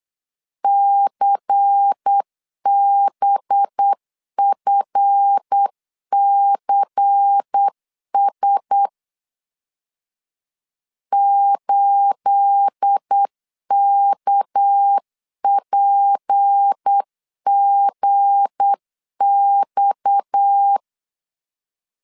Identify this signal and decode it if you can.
Morse code, heard as a keyed tone, about 8 wpm, 800 Hz. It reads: CBFCS 8KPGX
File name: CW / DET / 1 9 1500C S A 8wpm 5637